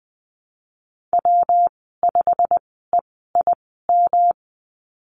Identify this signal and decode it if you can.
Morse code, heard as a keyed tone, about 19 wpm, spaced as for 14 wpm, 700 Hz. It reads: W5EIM
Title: 161 Lesson 26g - Letter g - Call Signs 20wpm